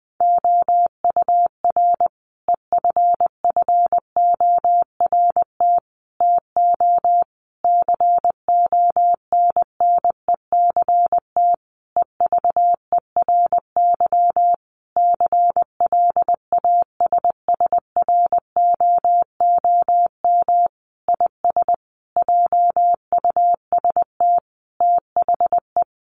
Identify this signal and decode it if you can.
Morse code, heard as a keyed tone, about 20 wpm, 700 Hz. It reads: OUR EFFORT TO CONNECT EVERY CLASSROOM IS JUST THE
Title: Sentences from Top 1500 Words 20wpm